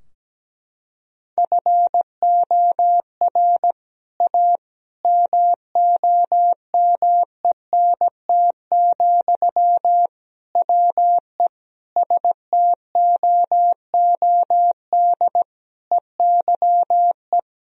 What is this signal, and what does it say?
Morse code, heard as a keyed tone, about 17 wpm, 700 Hz. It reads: FOR A MOMENT, WE STOOD EYE
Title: Sentences from Top 1000 Words - Review Part 2 - 17wpm